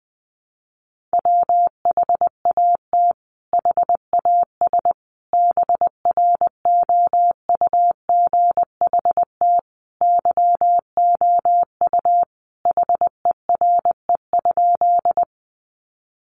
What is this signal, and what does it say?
Morse code, heard as a keyed tone, about 20 wpm, 700 Hz. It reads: WHAT HAS BROUGHT YOU HERE?
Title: Sentences from Top 500 Words 20wpm